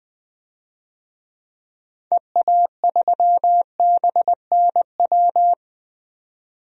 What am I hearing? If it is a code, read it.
Morse code, heard as a keyed tone, about 20 wpm, 700 Hz. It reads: EA3BNW
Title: DXCC-100 Call Signs - 6 Characters - 20wpm